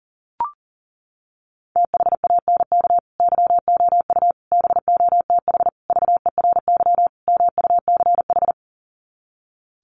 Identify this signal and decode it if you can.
Morse code, heard as a keyed tone, about 40 wpm, 700 Hz. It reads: THANK YOU BOTH VERY MUCH
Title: Sets of 5 Words 40wpm